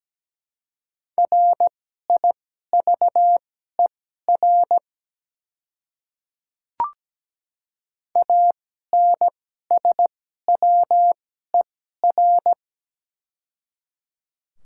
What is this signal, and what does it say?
Morse code, heard as a keyed tone, about 17 wpm, spaced as for 12 wpm, 690 Hz. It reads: RIVER ANSWER
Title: 2x Character Spacing - Top 300 Words - 17wpm